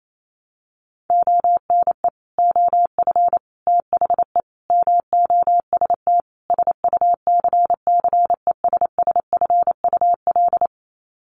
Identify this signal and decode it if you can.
Morse code, heard as a keyed tone, about 28 wpm, 700 Hz. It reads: ONE OF THE MOST SUCCESSFUL
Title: Sets of 5 Words Encore2 28wpm